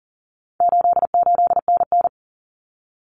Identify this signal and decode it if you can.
Morse code, heard as a keyed tone, about 40 wpm, 700 Hz. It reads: 88NN
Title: Common Cut Numbers - 4-Digit 40wpm